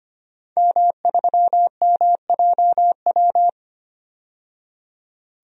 Morse code, heard as a keyed tone, about 25 wpm, 700 Hz. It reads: M3MJW
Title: DXCC-100 Call Signs - 5 Characters - 25wpm